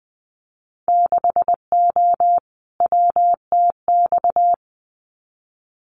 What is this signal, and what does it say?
Morse code, heard as a keyed tone, about 20 wpm, 700 Hz. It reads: 6O WTX
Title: Winter Field Day Exchanges Encore 20wpm